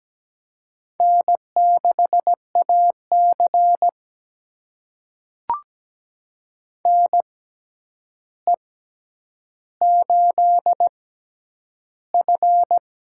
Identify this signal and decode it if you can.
Morse code, heard as a keyed tone, about 17 wpm, spaced as for 9 wpm, 700 Hz. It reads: N6AC NE8F
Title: USA Call Signs - 4 Characters - 6x Character Spacing - 17wpm